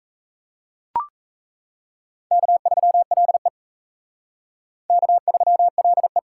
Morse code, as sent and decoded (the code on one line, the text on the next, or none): -.- ...-- .-.. / -.- ...-- .-.. .
K3LE K3LE